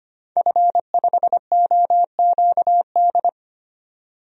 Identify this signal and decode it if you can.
Morse code, heard as a keyed tone, about 25 wpm, 700 Hz. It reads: F5OQD
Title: DXCC-100 Call Signs - 5 Characters - 25wpm